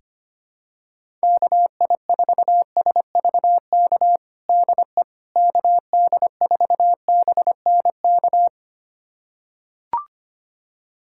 Morse code, heard as a keyed tone, about 25 wpm, 700 Hz. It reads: KI4SVK DE KD4BNK
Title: USA Callsign DE Callsign - 6 Characters - 25wpm